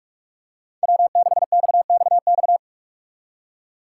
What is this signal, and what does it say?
Morse code, heard as a keyed tone, about 42 wpm, 700 Hz. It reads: W6XXX
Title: USA Call Signs - 5 Characters - 6x Character Spacing - 45wpm